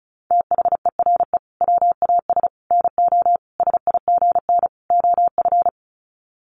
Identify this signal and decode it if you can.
Morse code, heard as a keyed tone, about 34 wpm, 700 Hz. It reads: THERE WAS NO SIGN OF
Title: Sets of 5 Words Encore2 35wpm